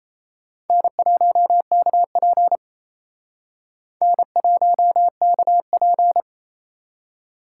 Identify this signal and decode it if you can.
Morse code, heard as a keyed tone, about 36 wpm, 700 Hz. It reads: N1KP N1KP